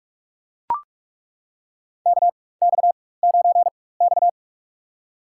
Morse code, heard as a keyed tone, about 41 wpm, spaced as for 20 wpm, 700 Hz. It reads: KX9X
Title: USA Call Signs - 4 Characters - 4x Character Spacing - 45wpm